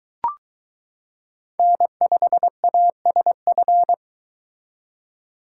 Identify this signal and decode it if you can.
Morse code, heard as a keyed tone, about 22 wpm, 700 Hz. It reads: N5ASF